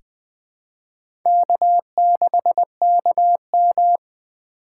Morse code, heard as a keyed tone, about 20 wpm, 700 Hz. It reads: K6KM
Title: 2023 SOTA Chaser Call Signs - 20wpm